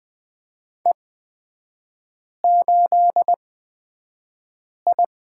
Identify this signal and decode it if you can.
Morse code, heard as a keyed tone, about 20 wpm, spaced as for 5 wpm, 700 Hz. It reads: E8I
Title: 255 Lesson 41e - Period - Three Characters 20wpm